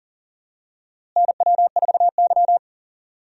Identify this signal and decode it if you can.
Morse code, heard as a keyed tone, about 40 wpm, 700 Hz. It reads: NW4Y
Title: USA Call Signs - 4 Characters - Repeated 2x - 40wpm